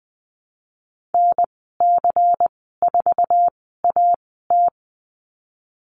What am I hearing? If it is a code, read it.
Morse code, heard as a keyed tone, about 20 wpm, spaced as for 15 wpm, 700 Hz. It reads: NC4AT
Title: 147 Lesson 24g - Letter y - Call Signs 20wpm